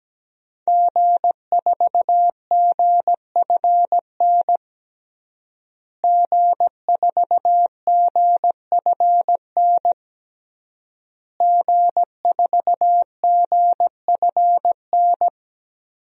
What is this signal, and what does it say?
Morse code, heard as a keyed tone, about 17 wpm, 700 Hz. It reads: G4GFN G4GFN G4GFN
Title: DXCC-100 Call Signs - 5 Characters - Repeated 3x - 17wpm